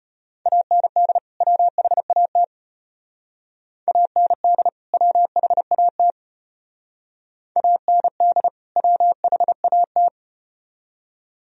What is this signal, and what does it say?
Morse code, heard as a keyed tone, about 32 wpm, 700 Hz. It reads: AND WHAT AND WHAT AND WHAT